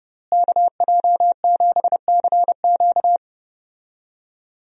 Morse code, heard as a keyed tone, about 31 wpm, 700 Hz. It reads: KJ7CQ
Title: USA Call Signs - 5 Characters - Repeated 3x - 30wpm